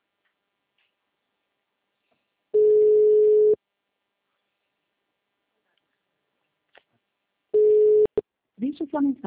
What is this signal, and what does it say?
Ringback tone (Europe, 425 Hz), with speech.